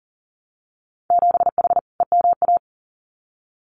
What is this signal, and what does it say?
Morse code, heard as a keyed tone, about 42 wpm, 700 Hz. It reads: 7H EMA
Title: Winter Field Day Exchanges Encore 40wpm